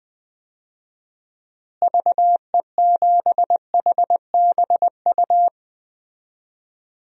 Morse code, heard as a keyed tone, about 20 wpm, 700 Hz. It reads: VE7HBU
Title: DXCC-100 Call Signs - 6 Characters - Repeated 2x - 20wpm